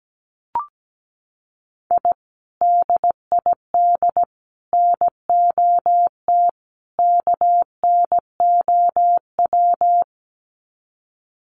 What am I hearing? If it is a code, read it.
Morse code, heard as a keyed tone, about 17 wpm, 700 Hz. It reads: I DID NOT KNOW
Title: Sets of 4 Words 17wpm